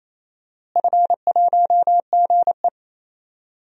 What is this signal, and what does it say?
Morse code, heard as a keyed tone, about 28 wpm, 700 Hz. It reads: F1GE
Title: DXCC-100 Call Signs - 4 Characters - Repeated 3x - 28wpm